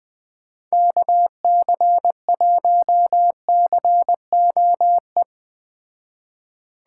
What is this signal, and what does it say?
Morse code, heard as a keyed tone, about 19 wpm, 700 Hz. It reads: KC1COE
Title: Speed Racing - USA Call Signs 17wpm